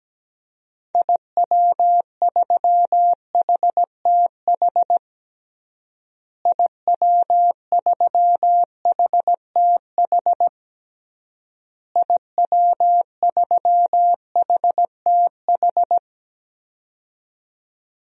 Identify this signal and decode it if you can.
Morse code, heard as a keyed tone, about 17 wpm, 700 Hz. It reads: IW3HTH IW3HTH IW3HTH
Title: DXCC-100 Call Signs - 6 Characters - Repeated 3x - 17wpm